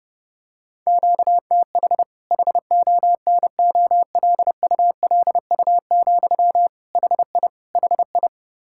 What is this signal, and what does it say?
Morse code, heard as a keyed tone, about 30 wpm, 700 Hz. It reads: QTH HONOLULU, HI HI
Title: Practice QSO - Exchange 1 - Beginning - 30wpm